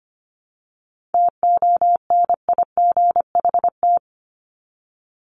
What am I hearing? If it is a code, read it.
Morse code, heard as a keyed tone, about 25 wpm, 700 Hz. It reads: TONIGHT
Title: Top 501-750 Words 25wpm